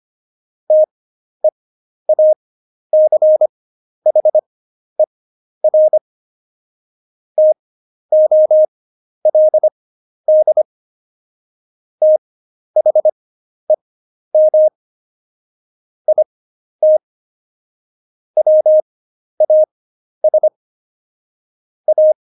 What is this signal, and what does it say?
Morse code, heard as a keyed tone, about 25 wpm, spaced as for 11 wpm, 620 Hz. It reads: TEACHER TOLD THEM IT WAS A
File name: Jokes-25-char-10-eff